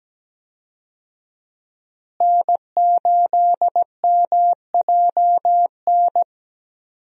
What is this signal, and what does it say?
Morse code, heard as a keyed tone, about 17 wpm, 700 Hz. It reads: N8MJN
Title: USA Call Signs - 5 Characters - Repeated 3x - 17wpm